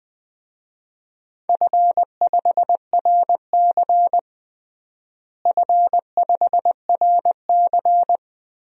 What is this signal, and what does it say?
Morse code, heard as a keyed tone, about 20 wpm, 700 Hz. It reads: F5RC F5RC
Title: DXCC-100 Call Signs - 4 Characters - Repeated 2x - 20wpm